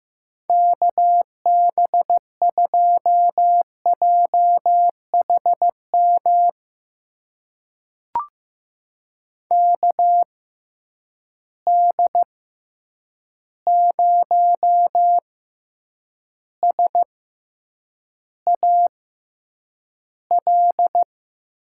Morse code, heard as a keyed tone, about 15 wpm, spaced as for 8 wpm, 700 Hz. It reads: KB2JHM KD0SAL
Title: USA Call Signs - 6 Characters - 6x Character Spacing - 15wpm